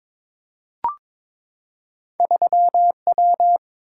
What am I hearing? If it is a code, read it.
Morse code, heard as a keyed tone, about 22 wpm, 700 Hz. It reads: 3W
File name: practice / 22wpm / Two Letter-Number Combinations - 22wpm